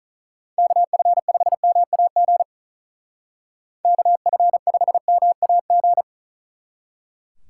Morse code, heard as a keyed tone, about 47 wpm, 700 Hz. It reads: KF5MAG KF5MAG